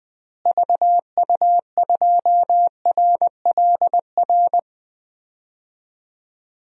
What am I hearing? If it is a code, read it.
Morse code, heard as a keyed tone, about 20 wpm, 700 Hz. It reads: VU2RLR
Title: DXCC-100 Call Signs - 6 Characters - Repeated 3x - 20wpm